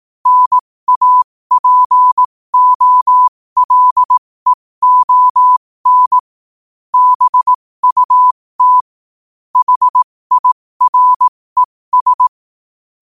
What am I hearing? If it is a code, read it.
Morse code, heard as a keyed tone, about 18 wpm, spaced as for 15 wpm, 1000 Hz. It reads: NAPOLEON BUT HIRES
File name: qotdfiles / 15wpm